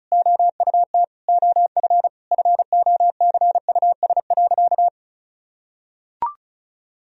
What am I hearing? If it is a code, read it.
Morse code, heard as a keyed tone, about 34 wpm, 700 Hz. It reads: OUT OF FOCUS.
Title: Sentences from Top 1000 Words - Review Part 4 - 35wpm